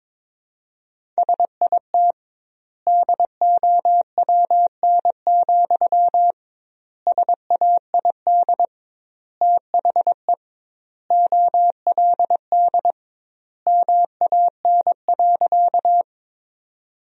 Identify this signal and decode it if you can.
Morse code, heard as a keyed tone, about 22 wpm, 700 Hz. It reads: SIT DOWN, SAID THE OLD MAN.
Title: 2x Word-Spacing - Sentences from Top 500 Words - 22wpm